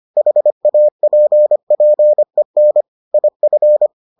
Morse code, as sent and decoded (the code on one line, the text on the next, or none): .... .- .--. .--. -. / .. ..-.
HAPPEN IF